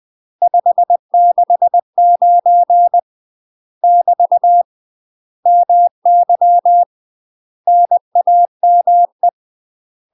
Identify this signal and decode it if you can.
Morse code, heard as a keyed tone, about 20 wpm, 700 Hz. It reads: 569 <BT> MY NAME
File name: UK MASTER QSOs TRAINING / 20WPM / UK MASTER QSOs 0011